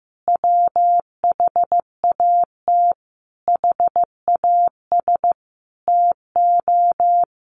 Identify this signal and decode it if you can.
Morse code, heard as a keyed tone, about 15 wpm, 700 Hz. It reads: WHAT HAS TO
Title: Sentences from Top 700 Words 15wpm